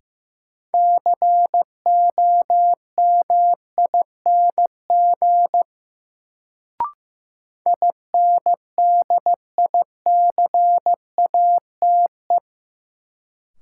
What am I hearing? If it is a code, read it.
Morse code, heard as a keyed tone, about 15 wpm, 700 Hz. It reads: COMING INDICATE
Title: Sentences from Top 1300 Words - New Word Drill Alt - 15wpm